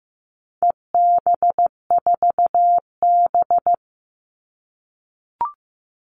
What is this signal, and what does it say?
Morse code, heard as a keyed tone, about 15 wpm, 700 Hz. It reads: EB4B
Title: All Cut Numbers - 4-Digit 15wpm